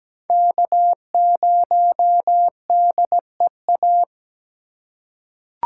Morse code, heard as a keyed tone, about 17 wpm, 700 Hz. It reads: K0DEA
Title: USA Call Signs - 5 Characters - 4x Character Spacing - 17wpm